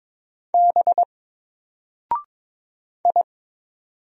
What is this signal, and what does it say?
Morse code, heard as a keyed tone, about 22 wpm, 700 Hz. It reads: B I
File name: Single-Letter-Number-22wpm